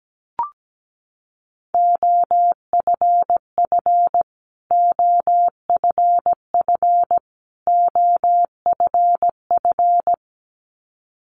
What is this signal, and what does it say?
Morse code, heard as a keyed tone, about 17 wpm, 700 Hz. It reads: OFF OFF OFF
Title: Top 300 Words Repeated 17wpm